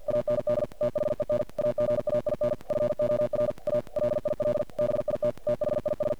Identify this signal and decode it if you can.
Morse code, heard as a keyed tone, about 45 wpm, 590 Hz, with noise. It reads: AND THEN AGAIN FOR A LIL BIT THIS
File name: archived / CRSnet